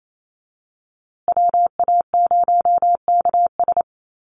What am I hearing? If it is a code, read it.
Morse code, heard as a keyed tone, about 28 wpm, 700 Hz. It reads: WA0KS
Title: USA Call Signs Repeated 3x 28wpm